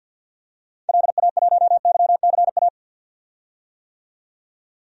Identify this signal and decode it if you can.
Morse code, heard as a keyed tone, about 47 wpm, 700 Hz. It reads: LA1YCA
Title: DXCC-100 Call Signs - 6 Characters - 50wpm